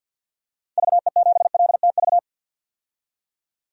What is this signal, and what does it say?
Morse code, heard as a keyed tone, about 49 wpm, 700 Hz. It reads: VE7LTV
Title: DXCC-100 Call Signs - 6 Characters - 50wpm